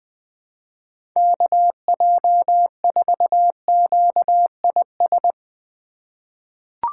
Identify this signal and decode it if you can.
Morse code, heard as a keyed tone, about 20 wpm, 700 Hz. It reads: KJ4QIS
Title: USA Call Signs - 6 Characters - 3x Character Spacing - 20wpm